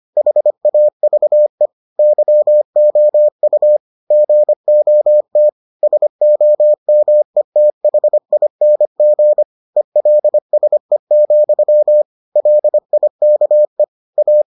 Morse code, heard as a keyed tone, about 25 wpm, 600 Hz. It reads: HAVE YOU GOT SOMETHING ELSE, LIKE A